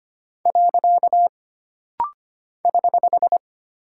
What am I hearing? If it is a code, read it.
Morse code, heard as a keyed tone, about 25 wpm, 700 Hz. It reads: . <HH>
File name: Single Punctuation-Prosign - Rapid-Fire - 25wpm